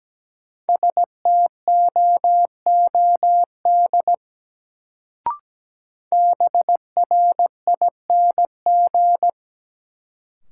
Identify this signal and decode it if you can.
Morse code, heard as a keyed tone, about 17 wpm, 700 Hz. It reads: STOOD BRING
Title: Sentences from Top 500 Words - New Word Drill - 17wpm